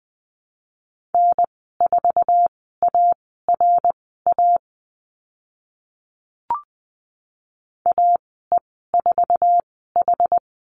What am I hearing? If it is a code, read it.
Morse code, heard as a keyed tone, about 20 wpm, spaced as for 14 wpm, 700 Hz. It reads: N4ARA AE4H
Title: 076 Lesson 13g - Letter L - Call Signs 20wpm